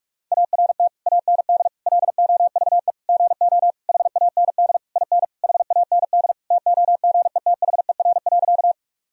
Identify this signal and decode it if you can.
Morse code, heard as a keyed tone, about 42 wpm, 700 Hz. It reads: ART AND LOVE GO HAND IN HAND TOGETHER.
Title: Sentences from Top 1000 Words - Review Part 3 - 45wpm